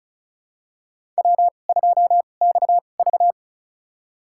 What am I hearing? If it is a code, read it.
Morse code, heard as a keyed tone, about 36 wpm, spaced as for 25 wpm, 700 Hz. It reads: W2XV
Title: USA Call Signs - 4 Characters - 2x Character Spacing - 35wpm